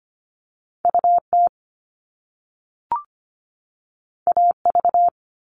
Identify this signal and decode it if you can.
Morse code, heard as a keyed tone, about 25 wpm, 700 Hz. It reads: UT AV
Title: Top Two Letter Combinations in Words 25wpm